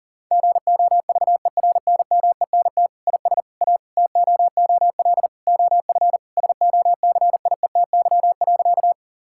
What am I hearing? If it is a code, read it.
Morse code, heard as a keyed tone, about 37 wpm, 700 Hz. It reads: GOVERNMENT IS A TOOL OF SOCIETY.
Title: Sentences from Top 1000 Words - Review Part 6 - 40wpm